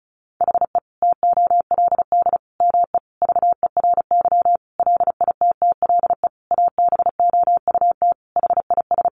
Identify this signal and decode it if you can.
Morse code, heard as a keyed tone, about 34 wpm, 700 Hz. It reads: HE TOLD ME VERY LITTLE ABOUT HIS
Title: Sentences from Top 1500 Words 35wpm